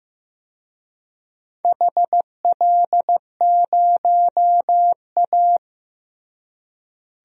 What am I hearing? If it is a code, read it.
Morse code, heard as a keyed tone, about 15 wpm, 700 Hz. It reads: HL0A
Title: DXCC-100 Call Signs - 4 Characters - Repeated 2x - 15wpm